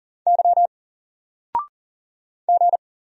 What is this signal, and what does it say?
Morse code, heard as a keyed tone, about 40 wpm, 700 Hz. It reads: Y G
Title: Single Character (Letter:Number:Punctuation:Prosign) 40wpm